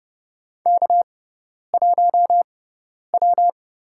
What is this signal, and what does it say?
Morse code, heard as a keyed tone, about 30 wpm, spaced as for 10 wpm, 700 Hz. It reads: K1W
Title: USA Call Signs - 4 Characters - 6x Character Spacing - 30wpm